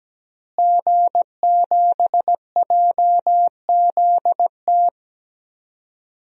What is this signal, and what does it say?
Morse code, heard as a keyed tone, about 17 wpm, 700 Hz. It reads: G7JZT
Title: DXCC-100 Call Signs - 5 Characters - Repeated 2x - 17wpm